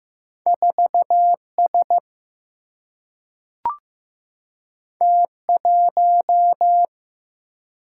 Morse code, heard as a keyed tone, about 15 wpm, 700 Hz. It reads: 4S T1
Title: Single-Letter Single-Number 15wpm